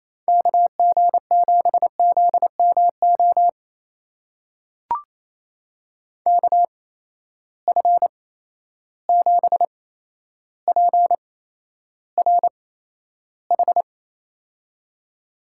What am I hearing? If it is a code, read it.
Morse code, heard as a keyed tone, about 28 wpm, spaced as for 12 wpm, 700 Hz. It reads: KG7ZMO KF7PRH
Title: USA Call Signs - 6 Characters - 8x Character Spacing - 28wpm